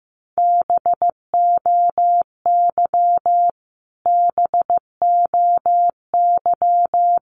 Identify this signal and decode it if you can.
Morse code, heard as a keyed tone, about 15 wpm, 700 Hz. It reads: BOY BOY
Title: Top 500 Words Repeated 15wpm